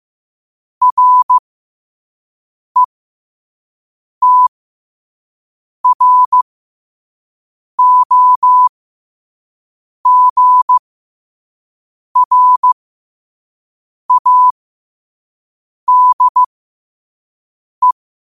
Morse code, heard as a keyed tone, about 15 wpm, spaced as for 5 wpm, 1000 Hz. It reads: RETROGRADE
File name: qotdfiles / 5wpm